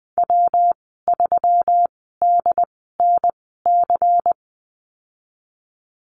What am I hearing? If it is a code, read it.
Morse code, heard as a keyed tone, about 20 wpm, spaced as for 14 wpm, 700 Hz. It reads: W3DNC
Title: 147 Lesson 24g - Letter y - Call Signs 20wpm